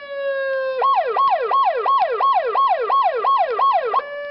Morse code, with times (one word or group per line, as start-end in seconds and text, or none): none